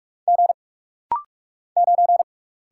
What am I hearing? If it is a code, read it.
Morse code, heard as a keyed tone, about 44 wpm, 700 Hz. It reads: G 9
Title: Single Letter-Number Alt - Rapid-Fire 45wpm